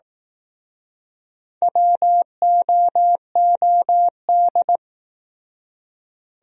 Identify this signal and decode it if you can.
Morse code, heard as a keyed tone, about 19 wpm, 700 Hz. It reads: WOOD
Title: Speed-Racing Top 500 Words 15wpm